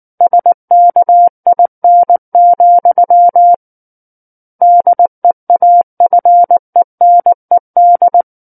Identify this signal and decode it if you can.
Morse code, heard as a keyed tone, about 19 wpm, 700 Hz. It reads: SKIN, DEAFENED
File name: Book One / 10-Chapter 10 - 19 WPM